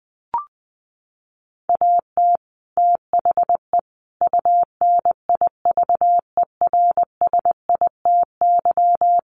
Morse code, heard as a keyed tone, about 20 wpm, 700 Hz. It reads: AT THE UNIVERSITY
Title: Sets of 5 Words Encore2 20wpm